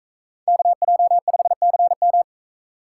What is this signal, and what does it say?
Morse code, heard as a keyed tone, about 40 wpm, 700 Hz. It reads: KJ5CM